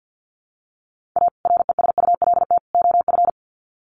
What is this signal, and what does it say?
Morse code, heard as a keyed tone, about 50 wpm, 700 Hz. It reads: A RESULT OF
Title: Sets of 3 Words 50wpm